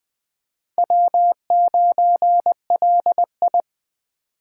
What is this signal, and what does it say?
Morse code, heard as a keyed tone, about 22 wpm, 700 Hz. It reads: W9LI